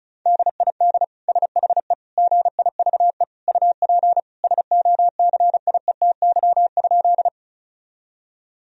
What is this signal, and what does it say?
Morse code, heard as a keyed tone, about 36 wpm, 700 Hz. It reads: DID SHE GIVE UP SOCIETY?
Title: Sentences from Top 1000 Words - Review Part 4 - 35wpm